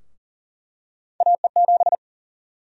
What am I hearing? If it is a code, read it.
Morse code, heard as a keyed tone, about 40 wpm, 700 Hz. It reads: AE7
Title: USA Call Sign Prefixes - Repeated 2x - 40wpm